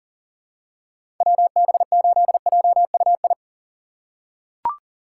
Morse code, heard as a keyed tone, about 40 wpm, 700 Hz. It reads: WB8JUI
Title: K1USN SST QSO Practice - 40wpm